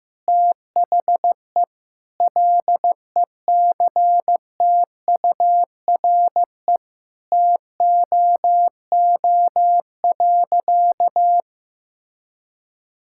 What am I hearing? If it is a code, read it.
Morse code, heard as a keyed tone, about 15 wpm, 700 Hz. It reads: THE LECTURE TOO.
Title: Sentences from Top 1600 Words 15wpm